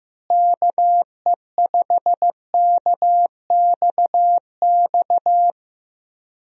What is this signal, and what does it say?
Morse code, heard as a keyed tone, about 15 wpm, 700 Hz. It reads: KE5KXX
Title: USA Call Signs - 6 Characters - Repeated 2x - 15wpm